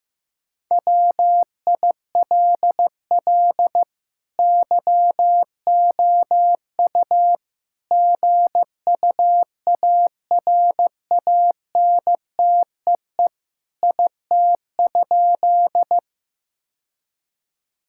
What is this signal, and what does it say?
Morse code, heard as a keyed tone, about 15 wpm, 700 Hz. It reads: WILL YOU GUARANTEE IT?
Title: Sentences from Top 1600 Words 15wpm